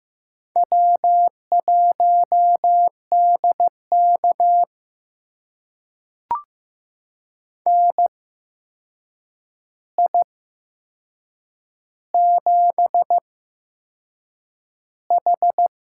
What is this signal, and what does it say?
Morse code, heard as a keyed tone, about 15 wpm, spaced as for 6 wpm, 700 Hz. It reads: W1DK NI7H
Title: USA Call Signs - 8x Character Spacing - 15wpm